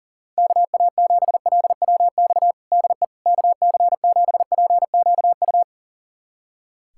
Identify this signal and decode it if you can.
Morse code, heard as a keyed tone, about 40 wpm, 700 Hz. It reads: KA7LWX DE KC7PQU
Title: USA Callsign DE Callsign - 6 Characters - 40wpm